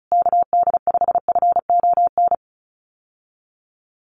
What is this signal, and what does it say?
Morse code, heard as a keyed tone, about 34 wpm, 700 Hz. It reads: KD5FON
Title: USA Call Signs Repeated 3x 35wpm